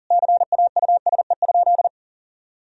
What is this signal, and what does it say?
Morse code, heard as a keyed tone, about 40 wpm, 700 Hz. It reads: CAUSE?
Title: Sentences from Top 1000 Words - Review Part 1 - 40wpm